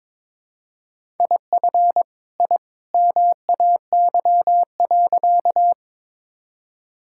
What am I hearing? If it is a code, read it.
Morse code, heard as a keyed tone, about 22 wpm, 700 Hz. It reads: IF I MAY.
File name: practice / 22wpm / Sentences with 2-Word Phrases - 22wpm